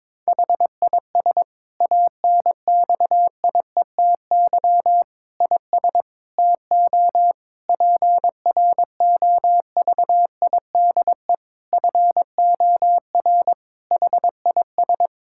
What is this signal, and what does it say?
Morse code, heard as a keyed tone, about 21 wpm, 700 Hz. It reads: HIS ANXIETY IS TO PROVIDE FOR HIS
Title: Sentences from Top 1600 Words 22wpm